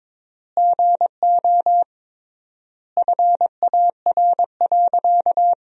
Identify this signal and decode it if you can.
Morse code, heard as a keyed tone, about 22 wpm, 700 Hz. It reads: GO FAR.
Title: 3x Word-Spacing - Sentences from Top 400 Words - 22wpm